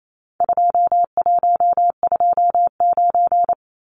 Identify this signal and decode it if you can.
Morse code, heard as a keyed tone, about 28 wpm, 700 Hz. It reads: 2129
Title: Numbers - 4-Digit 28wpm